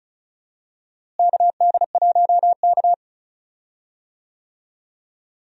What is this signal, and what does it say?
Morse code, heard as a keyed tone, about 36 wpm, 700 Hz. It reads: KD1K
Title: USA Call Signs - 4 Characters - Repeated 3x - 35wpm